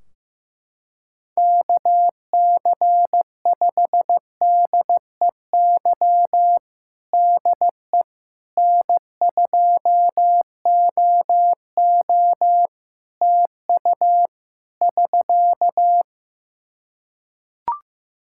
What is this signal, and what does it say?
Morse code, heard as a keyed tone, about 15 wpm, 700 Hz. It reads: KC5DEY DE N2OO TU <SK>
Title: K1USN SST QSO Practice 15wpm